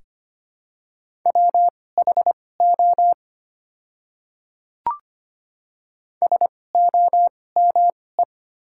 Morse code, heard as a keyed tone, about 25 wpm, spaced as for 18 wpm, 700 Hz. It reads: WHO SOME
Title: 2x Character Spacing - Top 100 Words - 25wpm